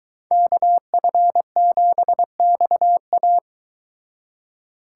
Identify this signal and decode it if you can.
Morse code, heard as a keyed tone, about 24 wpm, 700 Hz. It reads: KF7XA